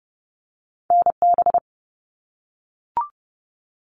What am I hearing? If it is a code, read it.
Morse code, heard as a keyed tone, about 30 wpm, 700 Hz. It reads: NB
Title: Canadian Provinces and Territories 30wpm